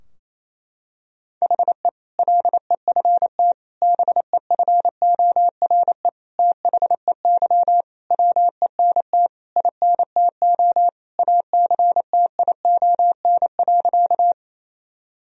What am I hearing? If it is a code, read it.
Morse code, heard as a keyed tone, about 28 wpm, 700 Hz. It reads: HE LEFT BEFORE THEY WENT INTO ACTION.
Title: Sentences from Top 1000 Words - Review Part 4 - 28wpm